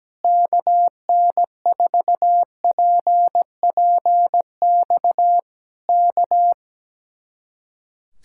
Morse code, heard as a keyed tone, about 17 wpm, 700 Hz. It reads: KN4PPX K
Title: Speed-Racing Common QSO Phrases - 17wpm